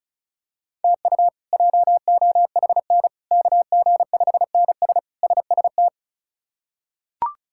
Morse code, heard as a keyed tone, about 36 wpm, 700 Hz. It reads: TU JOHN KG5NS SST
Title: K1USN SST QSO Practice - 35wpm